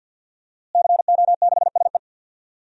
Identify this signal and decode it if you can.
Morse code, heard as a keyed tone, about 53 wpm, 700 Hz. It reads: CO6SE
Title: DXCC-100 Call Signs - 5 Characters - Repeated 2x - 50wpm